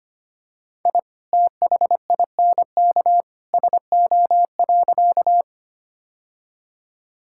Morse code, heard as a keyed tone, about 25 wpm, 690 Hz. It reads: I THINK SO.
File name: practice / 25wpm / Sentences with 2-Word Phrases - 25wpm